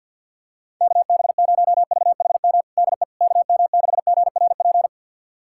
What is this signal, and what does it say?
Morse code, heard as a keyed tone, about 51 wpm, 700 Hz. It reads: KB0VHM DE KM6GRP
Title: USA Callsign DE Callsign - 6 Characters - 50wpm